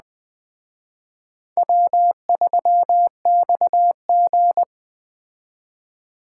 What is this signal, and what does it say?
Morse code, heard as a keyed tone, about 21 wpm, 700 Hz. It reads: W3XG
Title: Speed Racing - USA 1x2 Call Signs 17wpm